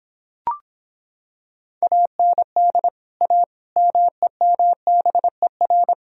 Morse code, heard as a keyed tone, about 26 wpm, 700 Hz. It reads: AND A MEMBER